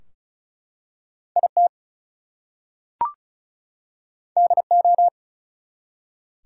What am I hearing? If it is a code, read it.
Morse code, heard as a keyed tone, about 36 wpm, 700 Hz. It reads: IT DO
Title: Two Letter Words - 35wpm